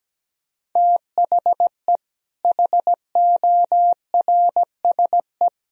Morse code, heard as a keyed tone, about 17 wpm, 700 Hz. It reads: THE HORSE
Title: Sentences from Top 1000 Words - Review Part 2 - 17wpm